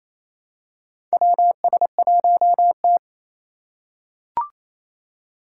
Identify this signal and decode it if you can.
Morse code, heard as a keyed tone, about 28 wpm, 700 Hz. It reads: WS1T
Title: USA Call Signs - 2x Character Spacing - 28wpm